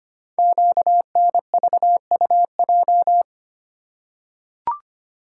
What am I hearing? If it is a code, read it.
Morse code, heard as a keyed tone, about 25 wpm, 700 Hz. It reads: QNVUJ
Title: Code Groups - 5 Letters - 25wpm